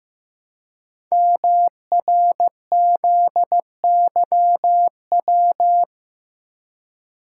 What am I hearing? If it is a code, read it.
Morse code, heard as a keyed tone, about 15 wpm, 700 Hz. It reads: MRZYW
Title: Code Groups - 5 Letters - 15wpm